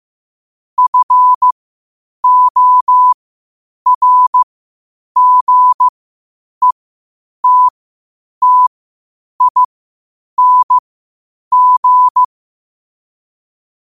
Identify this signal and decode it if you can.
Morse code, heard as a keyed tone, about 15 wpm, spaced as for 8 wpm, 1000 Hz. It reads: FORGETTING
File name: qotdfiles / 8wpm